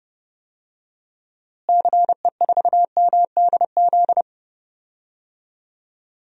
Morse code, heard as a keyed tone, about 32 wpm, 700 Hz. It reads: CE4MDZ
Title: DXCC-100 Call Signs - 6 Characters - Repeated 3x - 30wpm